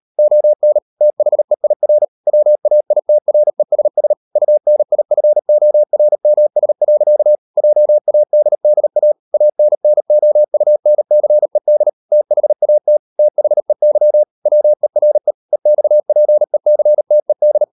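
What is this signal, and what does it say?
Morse code, heard as a keyed tone, about 38 wpm, 600 Hz. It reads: ON THEIR WAITRESS UNIFORMS. JADDA ANNOUNCED THAT THEY WERE EXPECTED
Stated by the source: Morse code practice at 38 WPM, sending Ken Follett's novel Never in English, from CW med Gnister.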